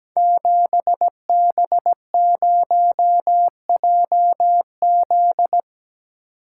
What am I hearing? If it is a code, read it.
Morse code, heard as a keyed tone, about 17 wpm, 700 Hz. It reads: 7B0JZ
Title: Code Groups - 5 Letter:Numbers - 17wpm